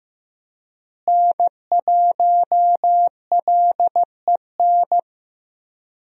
Morse code, heard as a keyed tone, about 15 wpm, 700 Hz. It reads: N1LEN
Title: USA Call Signs - 5 Characters - Repeated 3x - 15wpm